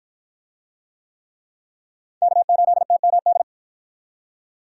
Morse code, heard as a keyed tone, about 53 wpm, 700 Hz. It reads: K8TMD